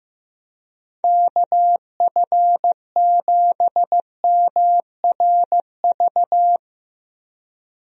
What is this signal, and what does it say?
Morse code, heard as a keyed tone, about 15 wpm, 700 Hz. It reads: KF7MRV
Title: USA Call Signs - 6 Characters - Repeated 2x - 15wpm